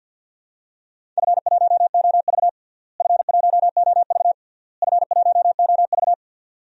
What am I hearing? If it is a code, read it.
Morse code, heard as a keyed tone, about 51 wpm, 700 Hz. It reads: F1OV F1OV F1OV
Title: DXCC-100 Call Signs - 4 Characters - Repeated 3x - 50wpm